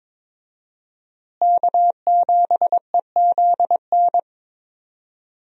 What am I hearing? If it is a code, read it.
Morse code, heard as a keyed tone, about 22 wpm, 700 Hz. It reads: K7EZN
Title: USA Call Signs - 5 Characters - Repeated 2x - 22wpm